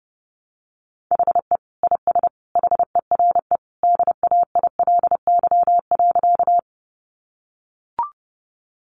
Morse code, heard as a keyed tone, about 30 wpm, 700 Hz. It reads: HE IS HERE DAILY.
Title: Sentences from Top 1100 Words 30wpm